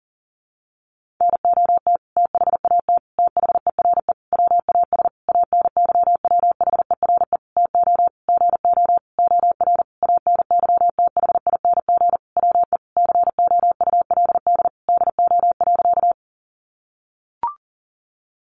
Morse code, heard as a keyed tone, about 40 wpm, 700 Hz. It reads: NOT THAT THERE WAS ANYWHERE TO GO OR ANYTHING WE COULD DO.
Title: Sentences from Top 1400 Words 40wpm